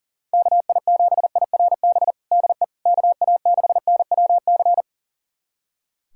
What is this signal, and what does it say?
Morse code, heard as a keyed tone, about 40 wpm, 700 Hz. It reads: KI7IRB DE KA6NWC
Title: USA Callsign DE Callsign - 6 Characters - 40wpm